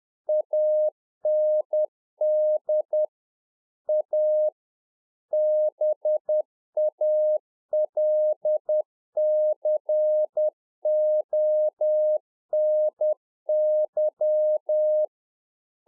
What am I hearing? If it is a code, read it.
Morse code, heard as a keyed tone, about 10 wpm, 610 Hz. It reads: AND A BALCONY
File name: 2011-04-mp3 / UNFOGETTABLE QSO 10WPM